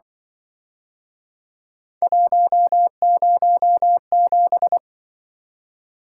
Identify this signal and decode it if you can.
Morse code, heard as a keyed tone, about 25 wpm, 700 Hz. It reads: 107